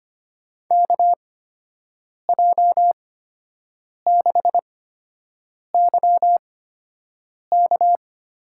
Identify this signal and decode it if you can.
Morse code, heard as a keyed tone, about 25 wpm, spaced as for 7 wpm, 700 Hz. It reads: KJ6YK
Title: USA Call Signs - 6 Characters - 8x Character Spacing - 25wpm